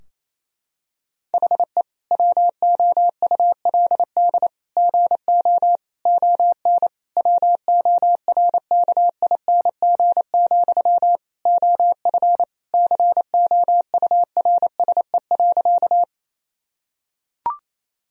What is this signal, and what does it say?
Morse code, heard as a keyed tone, about 27 wpm, 700 Hz. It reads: HE WOULD GO ON WORKING, OF COURSE.
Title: Sentences from Top 1000 Words - Review Part 4 - 28wpm